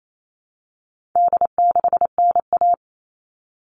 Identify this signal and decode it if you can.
Morse code, heard as a keyed tone, about 28 wpm, 700 Hz. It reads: D6NA
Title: All Cut Numbers - 4-Digit 28wpm